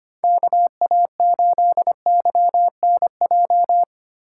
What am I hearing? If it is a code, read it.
Morse code, heard as a keyed tone, about 25 wpm, 700 Hz. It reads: KA8YNJ
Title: USA Call Signs - 6 Characters - 3x Character Spacing - 25wpm